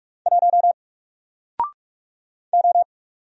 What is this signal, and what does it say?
Morse code, heard as a keyed tone, about 44 wpm, 700 Hz. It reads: 1 O
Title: Single Letter-Number Alt 45wpm